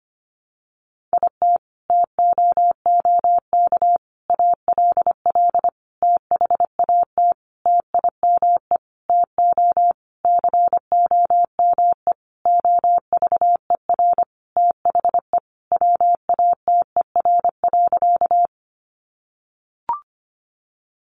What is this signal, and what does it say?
Morse code, heard as a keyed tone, about 25 wpm, 700 Hz. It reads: IT TOOK ALL THAT TIME TO COME OVER THE WATER.
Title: Sentences from Top 300 Words 25wpm